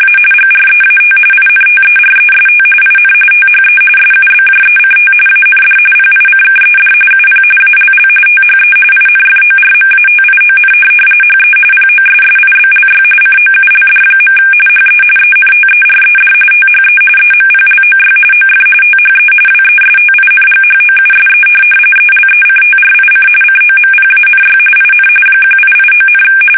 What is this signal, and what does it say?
RTTY, about 45.45 baud. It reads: A GREAT INTEREST IN THE SUBJECT AND PICKED UP ALL THE INFORMATION HE POSSIBLY COULD. HE ENTERED THE EMPLOY OF THE WESTINGHOUSE ELECTRIC COMPANY IN 1889,
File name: fsk850 / benjamin-g-lamme